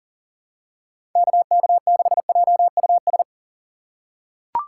Morse code, heard as a keyed tone, about 38 wpm, 700 Hz. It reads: KK6JUS
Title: USA Call Signs - 4x Character Spacing - 40wpm